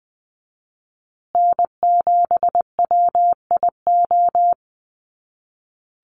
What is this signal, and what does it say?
Morse code, heard as a keyed tone, about 20 wpm, 700 Hz. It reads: N7WIO
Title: USA Call Signs Repeated 2x 20wpm